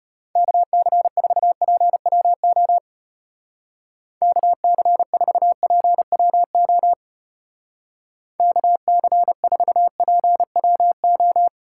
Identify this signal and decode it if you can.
Morse code, heard as a keyed tone, about 31 wpm, 700 Hz. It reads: KC4PWO KC4PWO KC4PWO
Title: Speed Racing - USA 2x3 Call Signs 25wpm